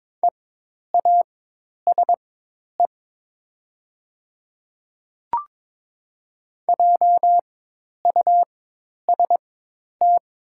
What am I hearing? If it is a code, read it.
Morse code, heard as a keyed tone, about 22 wpm, spaced as for 10 wpm, 700 Hz. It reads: EASE JUST
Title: 4x Character Spacing - Top 300 Words - 22wpm